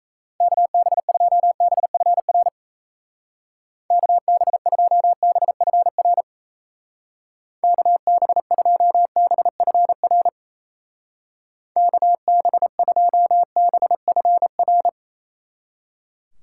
Morse code, heard as a keyed tone, about 36 wpm, 700 Hz. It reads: KB2BFR KB2BFR KB2BFR KB2BFR